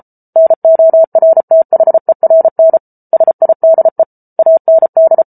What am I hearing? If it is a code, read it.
Morse code, heard as a keyed tone, about 34 wpm, 650 Hz. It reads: NORTHERN SIDE AND